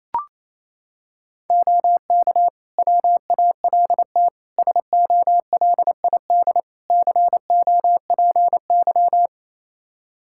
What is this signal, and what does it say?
Morse code, heard as a keyed tone, about 28 wpm, 700 Hz. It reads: OK WALT SOLID COPY
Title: Practice QSO - Exchange 3 - More Info - 28wpm